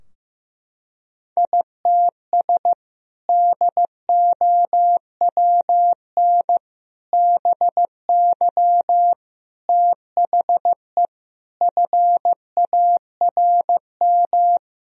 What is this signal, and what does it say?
Morse code, heard as a keyed tone, about 15 wpm, 700 Hz. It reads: ITS DOWN BY THE FARM
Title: Sentences from Top 1000 Words - Review Part 2 - 15wpm